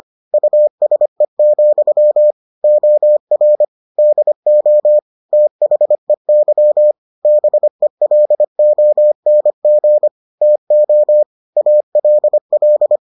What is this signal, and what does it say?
Morse code, heard as a keyed tone, about 25 wpm, 600 Hz. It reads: USE, OR DO THEY BELONG TO ALL